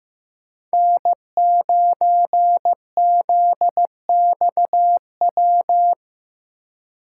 Morse code, heard as a keyed tone, about 15 wpm, 700 Hz. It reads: N9ZXW
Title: USA Call Signs - 3x Character Spacing - 15wpm